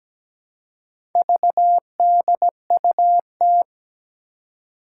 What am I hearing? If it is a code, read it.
Morse code, heard as a keyed tone, about 17 wpm, 700 Hz. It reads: VDUT
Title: Four Challenging Letter-Number Combinations 17wpm